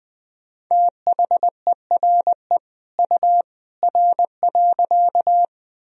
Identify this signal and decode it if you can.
Morse code, heard as a keyed tone, about 20 wpm, 700 Hz. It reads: THERE U R.
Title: Sentences from Top 100 Words and Abbreviations 20wpm